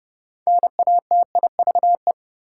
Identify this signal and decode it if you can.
Morse code, heard as a keyed tone, about 32 wpm, 700 Hz. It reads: NATIVE
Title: Six Letter Words - 30wpm